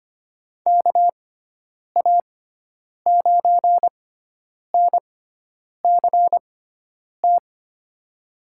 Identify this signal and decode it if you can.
Morse code, heard as a keyed tone, about 25 wpm, spaced as for 9 wpm, 700 Hz. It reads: KA9NCT
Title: USA Call Signs - 6 Characters - 6x Character Spacing - 25wpm